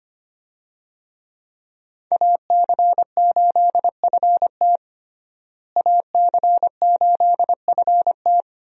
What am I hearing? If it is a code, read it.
Morse code, heard as a keyed tone, about 25 wpm, 700 Hz. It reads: AC8FT AC8FT
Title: USA Call Signs - 5 Characters - Repeated 3x - 25wpm